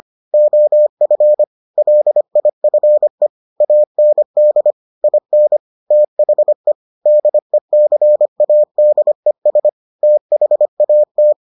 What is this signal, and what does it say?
Morse code, heard as a keyed tone, about 25 wpm, 600 Hz. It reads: OF LIFE AND IN THE DECADES THAT